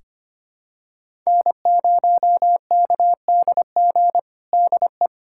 Morse code, heard as a keyed tone, about 25 wpm, 700 Hz. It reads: N0KDG DE
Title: USA Callsign DE Callsign 25wpm